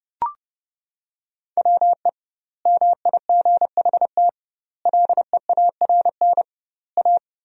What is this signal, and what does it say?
Morse code, heard as a keyed tone, about 32 wpm, 700 Hz. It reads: WE MIGHT LEARN A
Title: 2x Word-Spacing - Sentences from Top 300 Words - 30wpm